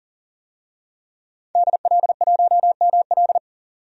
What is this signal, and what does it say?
Morse code, heard as a keyed tone, about 40 wpm, 700 Hz. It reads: DL1ML
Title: DXCC-100 Call Signs - 5 Characters - 40wpm